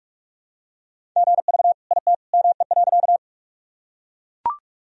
Morse code, heard as a keyed tone, about 44 wpm, 700 Hz. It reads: GV IT ME.
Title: Sentences from Top 200 Words and Abbreviations 45wpm